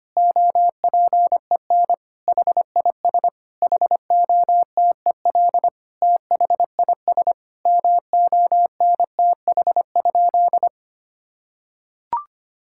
Morse code, heard as a keyed tone, about 25 wpm, 700 Hz. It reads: OPEN HIS HOTEL THIS MONTH?
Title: Sentences from Top 1600 Words 25wpm